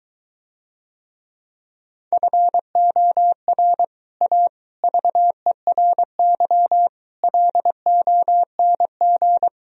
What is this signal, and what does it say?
Morse code, heard as a keyed tone, about 24 wpm, 700 Hz. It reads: FOR A VERY LONG